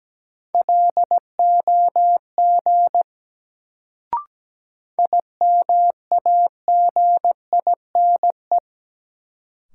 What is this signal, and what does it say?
Morse code, heard as a keyed tone, about 17 wpm, 700 Hz. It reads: LOG IMAGINE
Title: Sentences from Top 1400 Words - New Word Drill - 17wpm